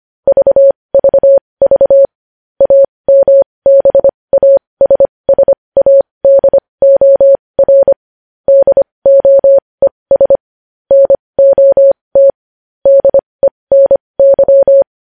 Morse code, heard as a keyed tone, about 25 wpm, spaced as for 20 wpm, 550 Hz. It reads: VVV AMBASSADOR DOES NOT DENY
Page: News Headlines in Morse Code at 20 WPM